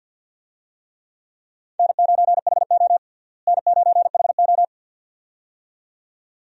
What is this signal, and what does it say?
Morse code, heard as a keyed tone, about 47 wpm, 700 Hz. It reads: N9HO N9HO
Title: USA Call Signs - 4 Characters - Repeated 2x - 50wpm